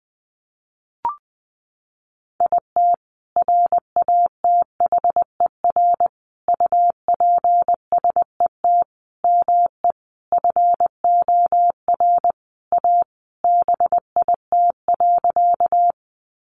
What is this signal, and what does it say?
Morse code, heard as a keyed tone, about 20 wpm, 700 Hz. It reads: IT RATHER UPSET ME FOR A BIT.
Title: Sentences from Top 900 Words 20wpm